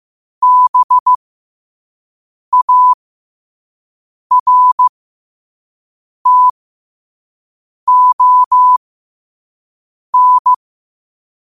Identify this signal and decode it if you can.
Morse code, heard as a keyed tone, about 15 wpm, spaced as for 5 wpm, 1000 Hz. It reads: BARTON
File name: 5wpm